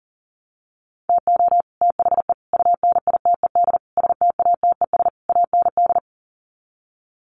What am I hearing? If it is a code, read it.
Morse code, heard as a keyed tone, about 40 wpm, 700 Hz. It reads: TO THE UNITED STATES AND
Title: Sets of 5 Words Encore 40wpm